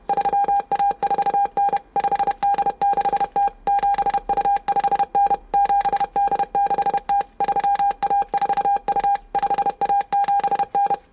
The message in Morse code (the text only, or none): 3A4N 5D6T 7U5N 7D6T 3A4U 5A7N